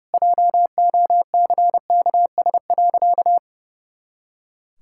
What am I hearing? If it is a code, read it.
Morse code, heard as a keyed tone, about 32 wpm, 700 Hz. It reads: JOCKS.
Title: Pangram Challenge v2 - 30wpm